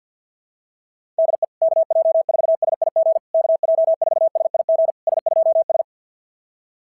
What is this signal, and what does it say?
Morse code, heard as a keyed tone, about 46 wpm, 660 Hz. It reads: DE KJ4SIG KJ4SIG S2S
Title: SOTA QSO Practice 50wpm